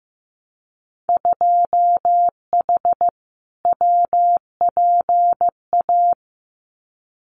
Morse code, heard as a keyed tone, about 15 wpm, 700 Hz. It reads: 2H WPA
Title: Winter Field Day Exchanges 15wpm